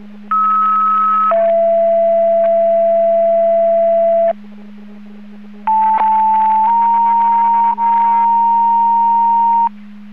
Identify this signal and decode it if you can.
Two-tone page, heard as a one-second tone, then a three-second tone, about 1290 then 670 Hz, about 880 then 930 Hz. It is heard from 0.3 to 9.7 s.